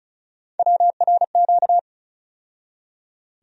Morse code, heard as a keyed tone, about 36 wpm, 700 Hz. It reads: WRQ
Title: USA Call Sign Postfixes - 3 Letters 35wpm